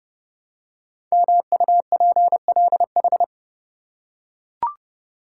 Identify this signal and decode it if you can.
Morse code, heard as a keyed tone, about 30 wpm, 690 Hz. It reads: MUPLH
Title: Code Groups - 5 Letters - 30wpm